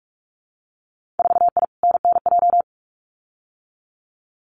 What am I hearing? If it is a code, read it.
Morse code, heard as a keyed tone, about 44 wpm, 700 Hz. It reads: 4I NNJ
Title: Winter Field Day Exchanges 45wpm